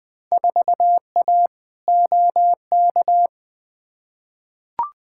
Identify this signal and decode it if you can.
Morse code, heard as a keyed tone, about 20 wpm, 700 Hz. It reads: 4A OK
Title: Field Day Exchanges - 20wpm